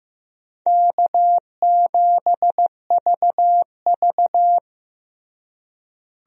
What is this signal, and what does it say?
Morse code, heard as a keyed tone, about 15 wpm, 700 Hz. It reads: K7VV
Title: USA Call Signs - 4 Characters - Repeated 3x - 15wpm